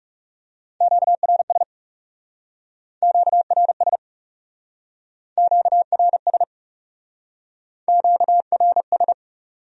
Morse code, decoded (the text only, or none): QRS QRS QRS QRS